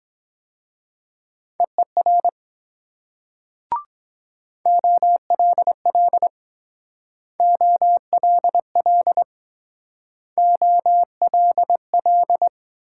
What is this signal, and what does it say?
Morse code, heard as a keyed tone, about 23 wpm, 700 Hz. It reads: EER OLL OLL OLL